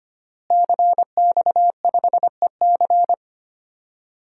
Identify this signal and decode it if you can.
Morse code, heard as a keyed tone, about 25 wpm, 700 Hz. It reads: CX5EC
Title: DXCC-100 Call Signs - 5 Characters - Repeated 3x - 25wpm